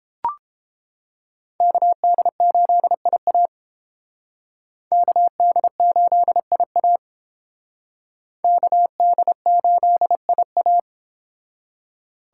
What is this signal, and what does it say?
Morse code, heard as a keyed tone, about 29 wpm, 700 Hz. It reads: KD8IA KD8IA KD8IA